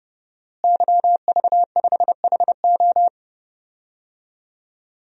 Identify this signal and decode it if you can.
Morse code, heard as a keyed tone, about 30 wpm, 700 Hz. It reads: YV5HO
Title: DXCC-100 Call Signs - 5 Characters - Repeated 2x - 30wpm